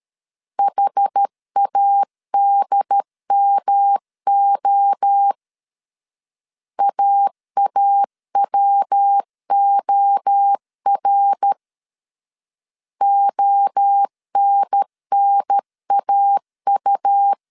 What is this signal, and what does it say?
Morse code, heard as a keyed tone, about 13 wpm, 800 Hz. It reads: HADMO AAWOR ONNAU
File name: CW / DET / 1 3 1500C S A 12wpm 2905